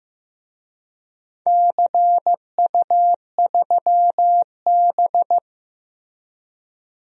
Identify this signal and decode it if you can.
Morse code, heard as a keyed tone, about 15 wpm, 700 Hz. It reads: CU3B
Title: DXCC-100 Call Signs - 4 Characters - 15wpm